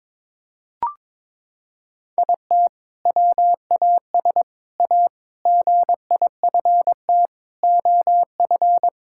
Morse code, heard as a keyed tone, about 22 wpm, 700 Hz. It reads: IT WAS A GIFT OF
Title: Sentences from Top 1000 Words - Review Part 6 - 22wpm